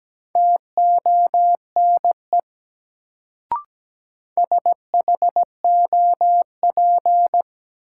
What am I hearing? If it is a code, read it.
Morse code, heard as a keyed tone, about 17 wpm, 700 Hz. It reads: TONE SHOP
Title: Sentences from Top 1600 Words - New Word Drill - 17wpm